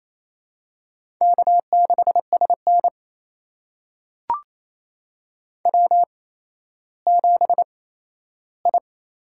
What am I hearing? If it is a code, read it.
Morse code, heard as a keyed tone, about 28 wpm, spaced as for 14 wpm, 700 Hz. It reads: K6SN W7I
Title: USA Call Signs - 4 Characters - 8x Character Spacing - 28wpm